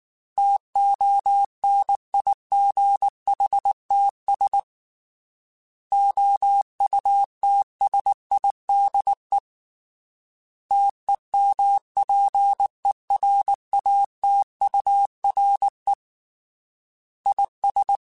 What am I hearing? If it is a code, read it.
Morse code, heard as a keyed tone, about 19 wpm, 800 Hz. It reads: TONIGHTS OUTSIDE TEMPERATURE IS
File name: Weather Sets / Weather Sets-mp3s / Weather 5 22 @18